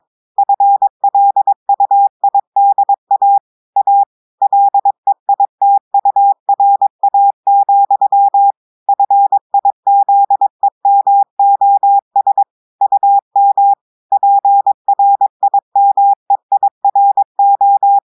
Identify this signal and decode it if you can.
Morse code, heard as a keyed tone, about 22 wpm, 800 Hz. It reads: FLUIDA A LEITURA, FIZEMOS UM PRIMEIRO
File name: audios / portugais / portugais texte en clair 1 vitesse 22 mots minutes